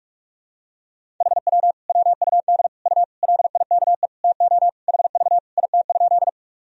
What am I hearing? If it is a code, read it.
Morse code, heard as a keyed tone, about 42 wpm, 700 Hz. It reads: HW WUD U LIKE TO HV IT?
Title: Sentences from Top 100 Words and Abbreviations 45wpm